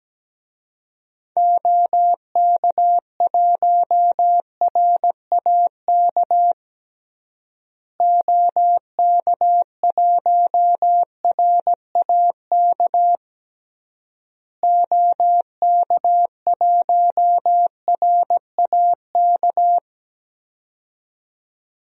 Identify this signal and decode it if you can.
Morse code, heard as a keyed tone, about 17 wpm, 700 Hz. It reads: OK1RAK OK1RAK OK1RAK
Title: DXCC-100 Call Signs - 6 Characters - Repeated 3x - 17wpm